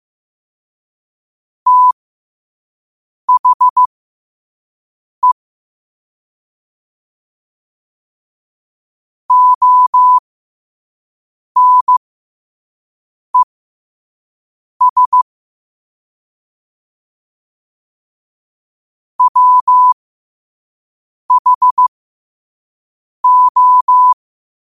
Morse code, heard as a keyed tone, about 15 wpm, spaced as for 5 wpm, 1000 Hz. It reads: THE ONES WHO
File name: qotdfiles / 5wpm